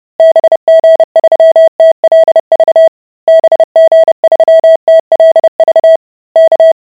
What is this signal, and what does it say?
Morse code, heard as a keyed tone, about 30 wpm, 650 Hz. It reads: BG3TLV BG3TLV K